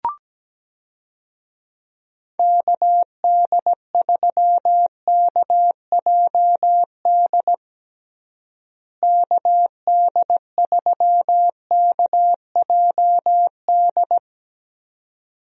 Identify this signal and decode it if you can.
Morse code, heard as a keyed tone, about 17 wpm, 700 Hz. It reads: KD3KJD KD3KJD
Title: USA Call Signs - 6 Characters - Repeated 2x - 17wpm